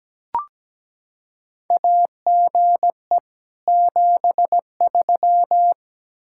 Morse code, heard as a keyed tone, about 17 wpm, 700 Hz. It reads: AGE 73
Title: Practice QSO - Exchange 3 - More Info - 17wpm